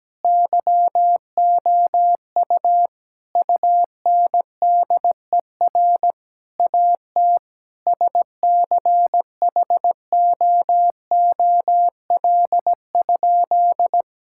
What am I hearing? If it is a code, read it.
Morse code, heard as a keyed tone, about 17 wpm, 700 Hz. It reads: YOU UNDER AT SCHOOL?
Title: Sentences from Top 1000 Words - Review Part 6 - 17wpm